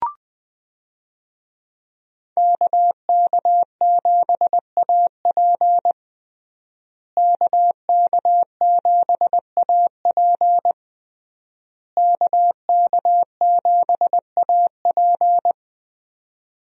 Morse code, heard as a keyed tone, about 20 wpm, 700 Hz. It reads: KK7AP KK7AP KK7AP
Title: USA Call Signs - 5 Characters - Repeated 3x - 20wpm